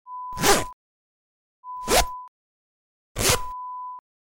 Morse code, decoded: none